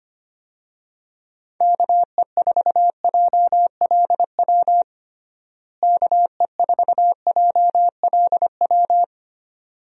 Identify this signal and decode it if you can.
Morse code, heard as a keyed tone, about 25 wpm, 700 Hz. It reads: KE4JLW KE4JLW
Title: USA Call Signs - 6 Characters - Repeated 2x - 25wpm